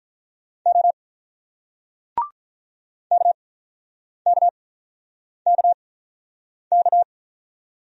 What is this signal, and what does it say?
Morse code, heard as a keyed tone, about 46 wpm, spaced as for 9 wpm, 700 Hz. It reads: O KKKK